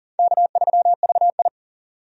Morse code, heard as a keyed tone, about 40 wpm, 700 Hz. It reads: K3VI
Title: USA Call Signs - 4 Characters - 40wpm